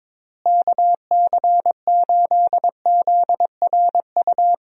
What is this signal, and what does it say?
Morse code, heard as a keyed tone, about 22 wpm, 700 Hz. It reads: KC8ZRU
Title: USA Call Signs - 6 Characters - Repeated 3x - 22wpm